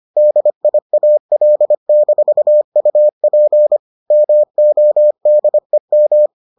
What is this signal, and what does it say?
Morse code, heard as a keyed tone, about 25 wpm, 600 Hz. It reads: DIAL-UP MODEM